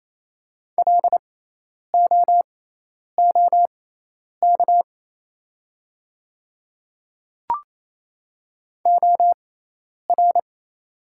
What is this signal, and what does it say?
Morse code, heard as a keyed tone, about 27 wpm, spaced as for 10 wpm, 700 Hz. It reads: LOOK OR
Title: 6x Character Spacing - Top 100 Words - 28wpm